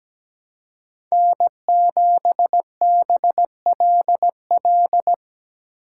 Morse code, heard as a keyed tone, about 17 wpm, 700 Hz. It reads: N7BLL
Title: USA Call Signs - 5 Characters - 3x Character Spacing - 17wpm